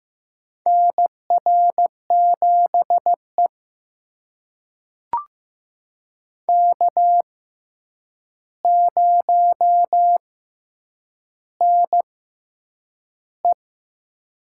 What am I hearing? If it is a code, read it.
Morse code, heard as a keyed tone, about 15 wpm, spaced as for 8 wpm, 700 Hz. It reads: NR7E K0NE
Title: USA Call Signs - 4 Characters - 6x Character Spacing - 15wpm